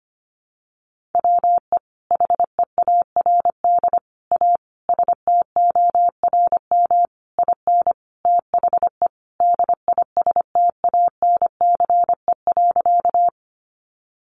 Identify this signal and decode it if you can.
Morse code, heard as a keyed tone, about 25 wpm, 700 Hz. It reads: WE HEARD A STORM IN THE DISTANCE.
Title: Sentences from Top 1500 Words 25wpm